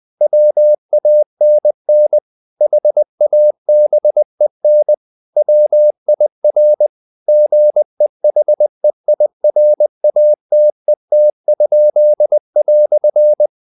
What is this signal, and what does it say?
Morse code, heard as a keyed tone, about 20 wpm, 600 Hz. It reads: WANN HABEN WIR GEHEIRATET?"
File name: dk5ke / Humor-100BpM